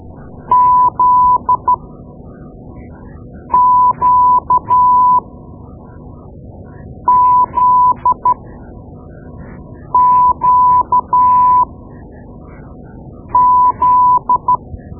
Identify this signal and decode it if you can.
Morse code, heard as a keyed tone, about 13 wpm, spaced as for 4 wpm, 1000 Hz. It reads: ZQZQZ
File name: Morse Lessons / Morse Lessons / Tape-2B